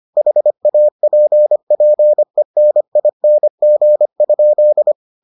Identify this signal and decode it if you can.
Morse code, heard as a keyed tone, about 25 wpm, 600 Hz. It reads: HAPPENING?